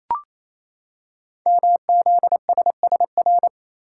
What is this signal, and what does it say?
Morse code, heard as a keyed tone, about 27 wpm, 700 Hz. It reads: MZSSR
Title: Code Groups - 5 Letters - 28wpm